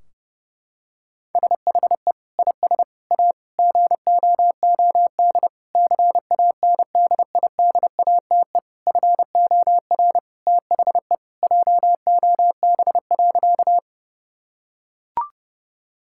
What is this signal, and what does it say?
Morse code, heard as a keyed tone, about 30 wpm, 700 Hz. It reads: SHE IS A GOOD CANDIDATE FOR THE JOB.
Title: Sentences from Top 1600 Words 30wpm